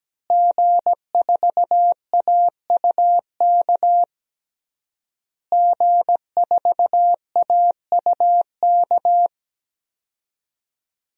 Morse code, heard as a keyed tone, about 17 wpm, 700 Hz. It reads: G4AUK G4AUK
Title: DXCC-100 Call Signs - 5 Characters - Repeated 3x - 17wpm